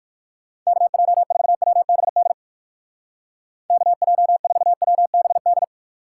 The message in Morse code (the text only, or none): KJ4WBD KJ4WBD